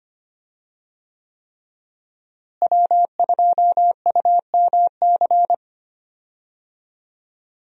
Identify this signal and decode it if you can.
Morse code, heard as a keyed tone, about 25 wpm, 700 Hz. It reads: W2UMC
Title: USA Call Signs - 5 Characters - Repeated 3x - 25wpm